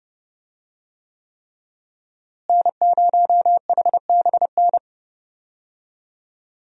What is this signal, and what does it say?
Morse code, heard as a keyed tone, about 31 wpm, 700 Hz. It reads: N0HBN
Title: USA Call Signs - 5 Characters - Repeated 3x - 30wpm